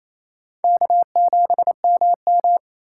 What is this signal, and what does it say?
Morse code, heard as a keyed tone, about 28 wpm, 700 Hz. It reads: K7MM
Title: USA Call Signs - 3x Character Spacing - 28wpm